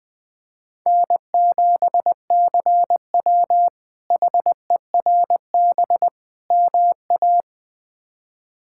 Morse code, heard as a keyed tone, about 20 wpm, 700 Hz. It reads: N7CW HERB MA
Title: K1USN SST QSO Practice - 20wpm